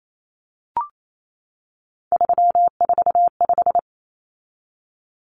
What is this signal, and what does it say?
Morse code, heard as a keyed tone, about 28 wpm, 700 Hz. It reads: 345
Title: Common Cut Numbers - 3-Digit 28wpm